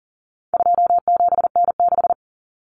Morse code, heard as a keyed tone, about 40 wpm, 700 Hz. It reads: 27N6
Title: Common Cut Numbers - 4-Digit 40wpm